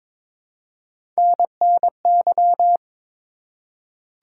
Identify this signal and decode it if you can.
Morse code, heard as a keyed tone, about 22 wpm, 700 Hz. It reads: NNY
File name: practice / 22wpm / ARRL Sections v2 - 22wpm